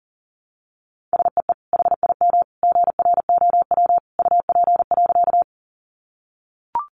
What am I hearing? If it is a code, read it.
Morse code, heard as a keyed tone, about 40 wpm, 700 Hz. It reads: SEE HIM GROW UP.
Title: Sentences from Top 300 Words 40wpm